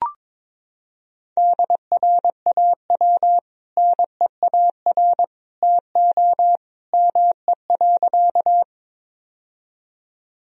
Morse code, heard as a keyed tone, about 22 wpm, 700 Hz. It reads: DRAW NEAR TO ME.
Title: Sentences from Top 1000 Words - Review Part 1 - 22wpm